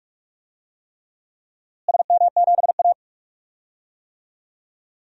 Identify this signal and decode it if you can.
Morse code, heard as a keyed tone, about 46 wpm, 700 Hz. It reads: SM7A
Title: DXCC-100 Call Signs - 4 Characters - Repeated 3x - 45wpm